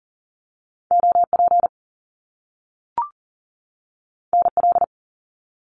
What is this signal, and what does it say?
Morse code, heard as a keyed tone, about 39 wpm, 700 Hz. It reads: OP NL
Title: Top Two Letter Combinations in Words 40wpm